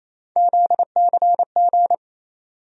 Morse code, heard as a keyed tone, about 28 wpm, 700 Hz. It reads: ZCG
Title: USA Call Sign Postfixes - 3 Letters 28wpm